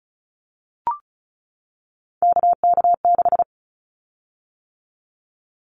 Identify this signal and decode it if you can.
Morse code, heard as a keyed tone, about 34 wpm, 700 Hz. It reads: KK6
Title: Callsign Prefixes for Top 25 DXCC - 35wpm